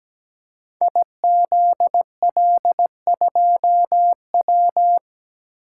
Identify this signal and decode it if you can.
Morse code, heard as a keyed tone, about 17 wpm, 700 Hz. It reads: IZL2W
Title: Code Groups - 5 Letter:Numbers - 17wpm